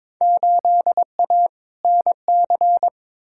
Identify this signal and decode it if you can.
Morse code, heard as a keyed tone, about 22 wpm, 700 Hz. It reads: 8A NC
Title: Field Day Exchanges - 22wpm